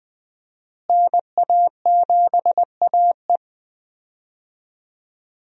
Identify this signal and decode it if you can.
Morse code, heard as a keyed tone, about 20 wpm, 700 Hz. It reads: NA7AE
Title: USA Call Signs - 5 Characters - Repeated 2x - 20wpm